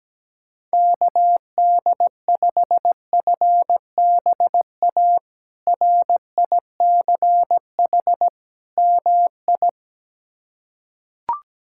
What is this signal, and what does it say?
Morse code, heard as a keyed tone, about 17 wpm, 700 Hz. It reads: KD5FBA RICH MI
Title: K1USN SST QSO Practice - 17wpm